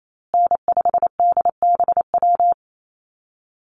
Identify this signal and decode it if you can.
Morse code, heard as a keyed tone, about 28 wpm, 700 Hz. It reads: N5DBW
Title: USA Call Signs Repeated 2x 28wpm